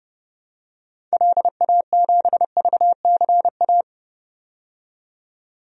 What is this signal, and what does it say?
Morse code, heard as a keyed tone, about 32 wpm, 700 Hz. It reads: LA7VCA
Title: DXCC-100 Call Signs - 6 Characters - 30wpm